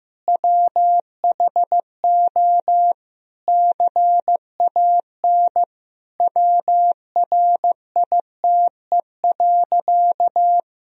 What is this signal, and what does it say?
Morse code, heard as a keyed tone, about 15 wpm, 700 Hz. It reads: WHO CAN WRITE.
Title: Sentences from Top 1000 Words - Review Part 1 - 15wpm